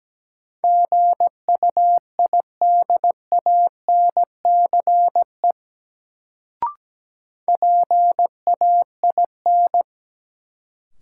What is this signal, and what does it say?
Morse code, heard as a keyed tone, about 17 wpm, 700 Hz. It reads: GUIDANCE PAIN
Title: Sentences from Top 1600 Words - New Word Drill Alt - 17wpm